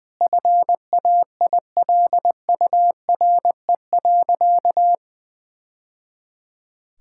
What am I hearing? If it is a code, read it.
Morse code, heard as a keyed tone, about 20 wpm, 700 Hz. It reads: FAILURE.
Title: Sentences from Top 1000 Words - Review Part 6 - 20wpm